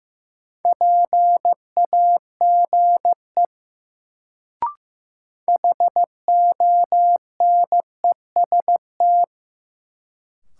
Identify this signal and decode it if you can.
Morse code, heard as a keyed tone, about 15 wpm, 700 Hz. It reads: PAGE HONEST
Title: Sets of 5 Words Encore - New Word Drill - 15wpm